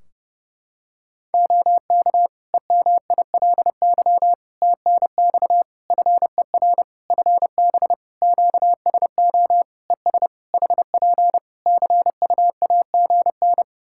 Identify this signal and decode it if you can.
Morse code, heard as a keyed tone, about 30 wpm, 700 Hz. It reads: OK EMILY TNX FER FB QSO ES HP CUAGN
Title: Practice QSO - Exchange 4 - Ending - 30wpm